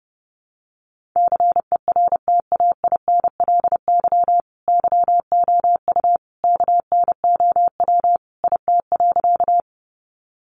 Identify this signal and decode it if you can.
Morse code, heard as a keyed tone, about 30 wpm, 700 Hz. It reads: CERTAINLY YOU KNOW IT.
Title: Sentences from Top 900 Words 30wpm